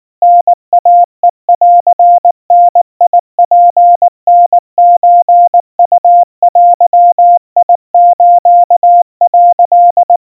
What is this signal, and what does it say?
Morse code (Japanese Wabun code), heard as a keyed tone, about 19 wpm, 700 Hz. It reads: タイヘンダツタソウデス」
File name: kawa / Morse sample / kamyu 70 cpm 700 Hz Crispy